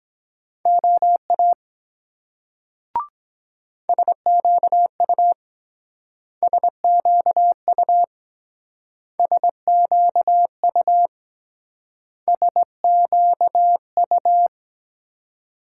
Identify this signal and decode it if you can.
Morse code, heard as a keyed tone, about 22 wpm, 700 Hz. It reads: OA SQU SQU SQU SQU